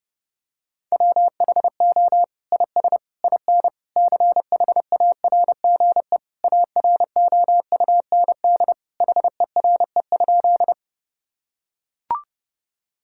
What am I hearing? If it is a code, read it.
Morse code, heard as a keyed tone, about 32 wpm, 700 Hz. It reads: WHO IS IN CHARGE AROUND HERE?
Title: Sentences from Top 1000 Words - Review Part 6 - 30wpm